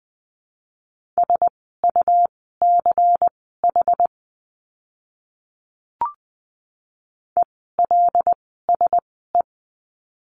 Morse code, heard as a keyed tone, about 20 wpm, spaced as for 15 wpm, 700 Hz. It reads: SUCH ELSE
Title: 101 Lesson 17f - Letter u - Words 20wpm